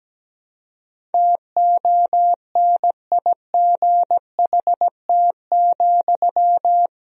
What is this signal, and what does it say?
Morse code, heard as a keyed tone, about 17 wpm, 700 Hz. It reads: TONIGHT,
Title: Sentences from Top 1000 Words - Review Part 3 - 17wpm